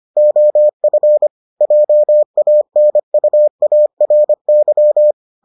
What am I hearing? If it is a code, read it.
Morse code, heard as a keyed tone, about 25 wpm, 600 Hz. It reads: OF JANUARY